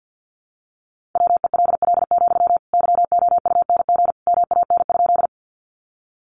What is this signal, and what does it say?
Morse code, heard as a keyed tone, about 50 wpm, 700 Hz. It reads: WELL, YOUNG MAN?
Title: Sentences from Top 400 Words 50wpm